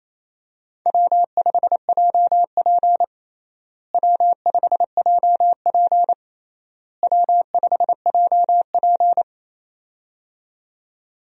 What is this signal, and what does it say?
Morse code, heard as a keyed tone, about 28 wpm, 700 Hz. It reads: W5JP W5JP W5JP
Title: USA Call Signs - 4 Characters - Repeated 3x - 28wpm